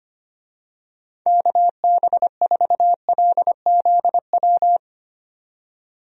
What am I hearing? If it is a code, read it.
Morse code, heard as a keyed tone, about 25 wpm, 700 Hz. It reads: KB4LZW
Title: USA Call Signs - 6 Characters - 25wpm